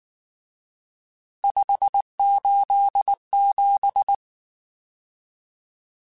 Morse code, heard as a keyed tone, about 19 wpm, 800 Hz. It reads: 587
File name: RSTs / RST-mp3s / Various RST @18